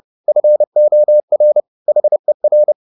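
Morse code, heard as a keyed tone, about 30 wpm, 600 Hz. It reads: FOR HER
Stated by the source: Morse code at 30 wpm, the novel Never by Ken Follett in English, from CW med Gnister.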